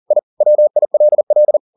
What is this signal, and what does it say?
Morse code, heard as a keyed tone, about 40 wpm, 600 Hz. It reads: I WILL
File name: the old man and the sea 2 40wpm